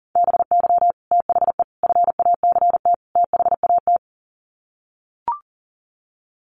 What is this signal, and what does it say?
Morse code, heard as a keyed tone, about 40 wpm, 700 Hz. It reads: BY THE FACT THAT
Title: Sets of 4 Words 40wpm